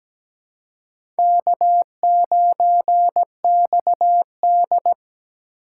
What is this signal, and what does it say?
Morse code, heard as a keyed tone, about 17 wpm, 700 Hz. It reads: K9XD
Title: K1USN SST QSO Practice - 17wpm